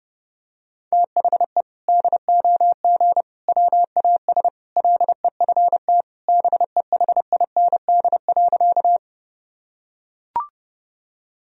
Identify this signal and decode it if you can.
Morse code, heard as a keyed tone, about 30 wpm, 700 Hz. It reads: THE DOG WAS LEFT BEHIND.
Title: Sentences from Top 1000 Words - Review Part 2 - 30wpm